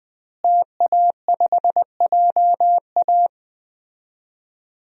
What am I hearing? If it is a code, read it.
Morse code, heard as a keyed tone, about 20 wpm, 700 Hz. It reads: TA5JA
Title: DXCC-100 Call Signs - 5 Characters - Repeated 2x - 20wpm